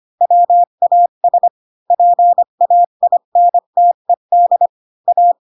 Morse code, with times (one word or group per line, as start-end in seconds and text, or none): 0.2-1.5 WAS
1.9-4.7 PAINTED
5.1-5.3 A